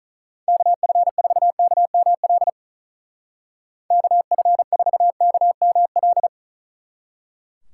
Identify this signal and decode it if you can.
Morse code, heard as a keyed tone, about 44 wpm, 700 Hz. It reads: KF4KML KF4KML